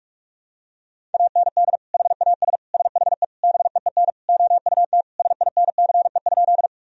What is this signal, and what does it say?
Morse code, heard as a keyed tone, about 46 wpm, 700 Hz. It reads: AND HAS SHE BEEN OUT SINCE?
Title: Sentences from Top 1000 Words - Review Part 1 - 45wpm